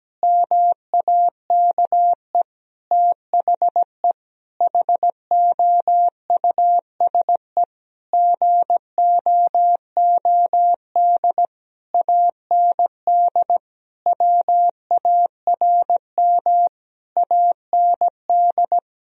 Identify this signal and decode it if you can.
Morse code, heard as a keyed tone, about 17 wpm, 700 Hz. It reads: MAKE THE HOUSE GOOD AND WARM AND
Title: Sentences from Top 1000 Words - Review Part 3 - 17wpm